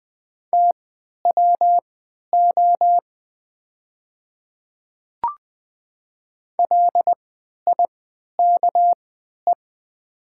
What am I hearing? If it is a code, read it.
Morse code, heard as a keyed tone, about 20 wpm, spaced as for 11 wpm, 700 Hz. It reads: TWO LIKE
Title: 3x Character Spacing - Top 100 Words - 20wpm